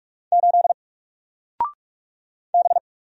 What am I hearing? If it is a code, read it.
Morse code, heard as a keyed tone, about 44 wpm, 700 Hz. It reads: 8 B
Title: Single Character (Letter:Number:Punctuation:Prosign) 45wpm